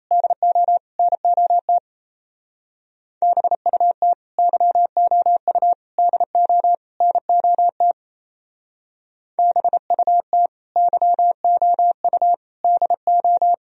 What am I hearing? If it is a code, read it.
Morse code, heard as a keyed tone, about 36 wpm, 700 Hz. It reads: DO NOT BUT YOU DO NOT BUT YOU DO